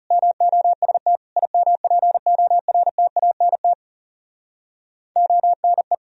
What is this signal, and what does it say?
Morse code, heard as a keyed tone, about 40 wpm, 700 Hz. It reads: MOST IMPORTANT ONE